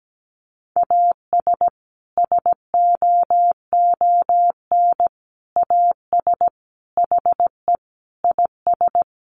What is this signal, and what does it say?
Morse code, heard as a keyed tone, about 17 wpm, 700 Hz. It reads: AS SOON AS HE IS
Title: Sentences from Top 1500 Words 17wpm